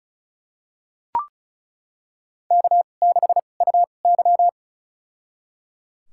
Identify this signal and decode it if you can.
Morse code, heard as a keyed tone, about 36 wpm, spaced as for 25 wpm, 700 Hz. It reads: K6UY
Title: USA Call Signs - 4 Characters - 2x Character Spacing - 35wpm